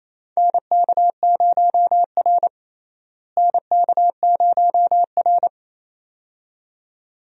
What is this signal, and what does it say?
Morse code, heard as a keyed tone, about 28 wpm, 700 Hz. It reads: NK0R NK0R
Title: USA Call Signs - 4 Characters - Repeated 2x - 28wpm